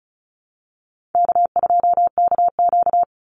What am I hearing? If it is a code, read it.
Morse code, heard as a keyed tone, about 34 wpm, 700 Hz. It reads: K2KQ
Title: Easy Call Signs Repeated 2x - 35wpm